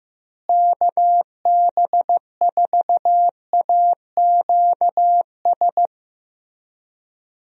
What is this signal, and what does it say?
Morse code, heard as a keyed tone, about 15 wpm, 700 Hz. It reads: KB4AQS
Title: USA Call Signs - 6 Characters - Repeated 2x - 15wpm